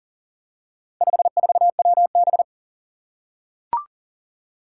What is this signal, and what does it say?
Morse code, heard as a keyed tone, about 41 wpm, 700 Hz. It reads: 54WB
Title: Four Challenging Letter-Number Combinations 40wpm